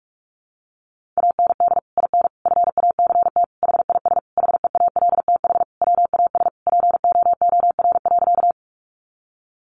Morse code, heard as a keyed tone, about 44 wpm, 700 Hz. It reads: AND IN FACT HIS HEALTH WAS POOR.
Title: Sentences from Top 1000 Words 45wpm